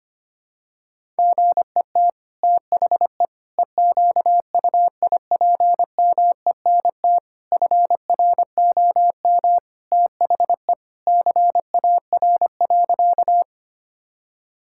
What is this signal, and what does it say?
Morse code, heard as a keyed tone, about 25 wpm, 700 Hz. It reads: GET THE EQUIPMENT FROM THE CAR.
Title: Sentences from Top 1000 Words - Review Part 5 - 25wpm